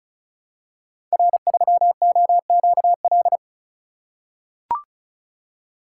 Morse code, heard as a keyed tone, about 36 wpm, 700 Hz. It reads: R3OQL
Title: Code Groups - 5 Letter:Numbers - 35wpm